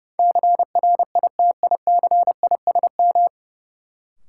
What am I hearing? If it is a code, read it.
Morse code, heard as a keyed tone, about 30 wpm, 700 Hz. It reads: CRITICISM
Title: Sentences from Top 1300 Words - New Word Drill Alt - 30wpm